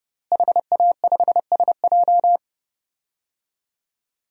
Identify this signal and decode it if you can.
Morse code, heard as a keyed tone, about 30 wpm, 700 Hz. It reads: HA5SJ
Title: DXCC-100 Call Signs - 5 Characters - Repeated 3x - 30wpm